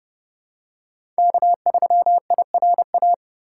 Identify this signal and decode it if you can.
Morse code, heard as a keyed tone, about 31 wpm, 700 Hz. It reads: K3IRA
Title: USA Call Signs - 5 Characters - 4x Character Spacing - 30wpm